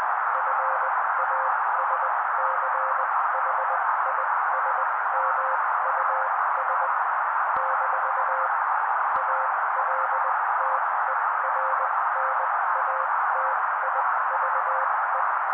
Morse code, heard as a keyed tone, about 20 wpm, spaced as for 16 wpm, 530 Hz. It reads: FASCHISMUS - ALTERNATIVE
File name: daily / 16wpm noise